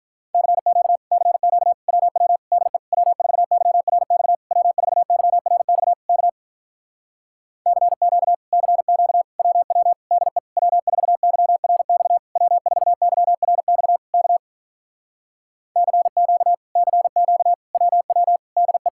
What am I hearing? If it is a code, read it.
Morse code, heard as a keyed tone, about 46 wpm, 700 Hz. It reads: CQ CQ WW DE W4YRX W4YRX K CQ CQ WW DE W4YRX W4YRX K CQ CQ WW DE